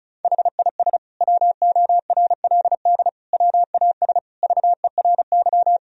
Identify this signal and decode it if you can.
Morse code, heard as a keyed tone, about 36 wpm, 700 Hz. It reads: HIS WORLD WAS VERY
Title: Sentences from Top 1000 Words - Review Part 1 - 35wpm